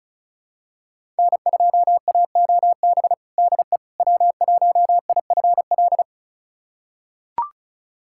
Morse code, heard as a keyed tone, about 36 wpm, 690 Hz. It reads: N2AOB DE W1IFL
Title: USA Callsign DE Callsign - 5 Characters - 35wpm